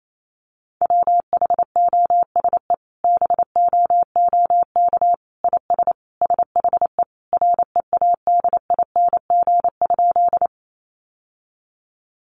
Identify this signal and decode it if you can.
Morse code, heard as a keyed tone, about 28 wpm, 700 Hz. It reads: WHOSE BOOK IS SHE READING?
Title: Sentences from Top 1200 Words 28wpm